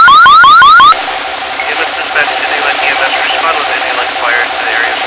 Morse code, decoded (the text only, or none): none